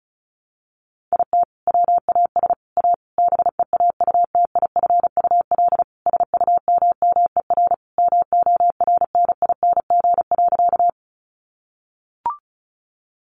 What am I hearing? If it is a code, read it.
Morse code, heard as a keyed tone, about 34 wpm, 700 Hz. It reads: IT WAS A BEAUTIFUL SUMMER MORNING.
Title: Sentences from Top 1100 Words 35wpm